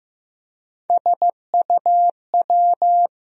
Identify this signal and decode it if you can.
Morse code, heard as a keyed tone, about 15 wpm, 700 Hz. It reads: SUW
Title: USA Call Sign Postfixes - 3 Letters 15wpm